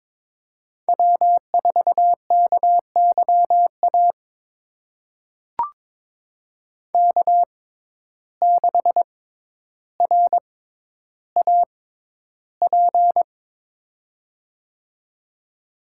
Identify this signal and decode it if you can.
Morse code, heard as a keyed tone, about 22 wpm, spaced as for 11 wpm, 700 Hz. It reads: W4KYA K6RAP
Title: USA Call Signs - 5 Characters - 6x Character Spacing - 22wpm